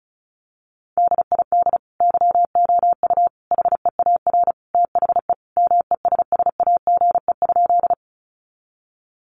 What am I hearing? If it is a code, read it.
Morse code, heard as a keyed tone, about 34 wpm, 700 Hz. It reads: DID YOU HEAR THE MESSAGE?
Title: Sentences from Top 1100 Words 35wpm